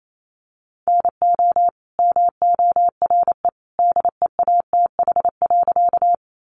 Morse code, heard as a keyed tone, about 28 wpm, 700 Hz. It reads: NO MORE DEATH.
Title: Sentences from Top 1100 Words 28wpm